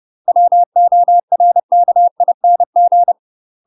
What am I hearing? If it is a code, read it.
Morse code, heard as a keyed tone, about 30 wpm, 700 Hz. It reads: WORKING
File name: UK MASTER QSOs TRAINING / 30WPM / UK MASTER QSOs 0043